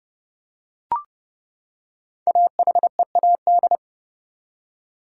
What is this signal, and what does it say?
Morse code, heard as a keyed tone, about 32 wpm, 700 Hz. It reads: AHEAD
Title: Five Letter Words - 30wpm